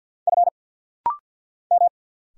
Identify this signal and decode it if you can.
Morse code, heard as a keyed tone, about 50 wpm, 690 Hz. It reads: F M
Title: Single Letter-Number - Rapid-Fire 50wpm